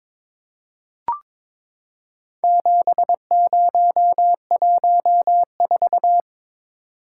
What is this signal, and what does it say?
Morse code, heard as a keyed tone, about 22 wpm, 700 Hz. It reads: 7014
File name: practice / 22wpm / Four Letter-Number Combinations - 22wpm